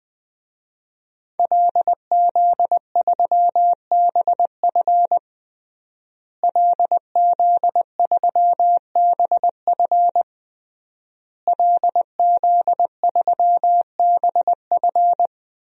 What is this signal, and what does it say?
Morse code, heard as a keyed tone, about 20 wpm, 700 Hz. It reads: LZ3BF LZ3BF LZ3BF
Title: DXCC-100 Call Signs - 5 Characters - Repeated 3x - 20wpm